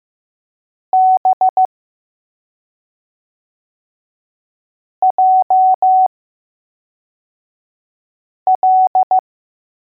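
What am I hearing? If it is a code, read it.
Morse code, heard as a keyed tone, about 15 wpm, spaced as for 3 wpm, 750 Hz. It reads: B JL
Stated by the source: Morse code letters test drill for L, J and B.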